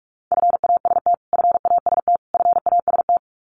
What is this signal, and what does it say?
Morse code, heard as a keyed tone, about 44 wpm, 700 Hz. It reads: FAST FAST FAST
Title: Top 500 Words Repeated 45wpm